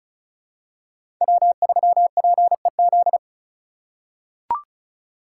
Morse code, heard as a keyed tone, about 36 wpm, 700 Hz. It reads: W3PEZ
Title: USA Call Signs - 2x Character Spacing - 35wpm